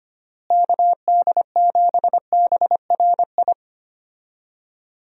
Morse code, heard as a keyed tone, about 25 wpm, 700 Hz. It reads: KD7BRI
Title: USA Call Signs - 6 Characters - Repeated 3x - 25wpm